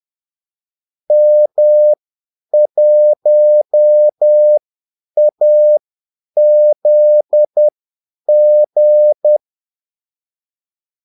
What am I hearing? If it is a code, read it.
Morse code, heard as a keyed tone, about 10 wpm, spaced as for 8 wpm, 600 Hz. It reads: M1AZG
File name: Callsigns-10-8wpm